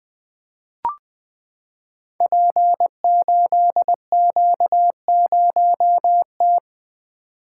Morse code, heard as a keyed tone, about 20 wpm, 700 Hz. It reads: P8Q0T
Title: Code Groups - 5 Letter:Numbers - 20wpm